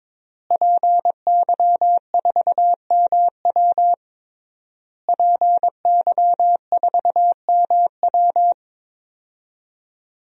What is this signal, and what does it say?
Morse code, heard as a keyed tone, about 22 wpm, 700 Hz. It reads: PY4MW PY4MW
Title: DXCC-100 Call Signs - 5 Characters - Repeated 3x - 22wpm